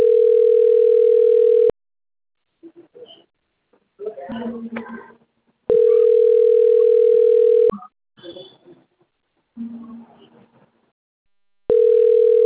Ringback tone (North America), 440 + 480 Hz.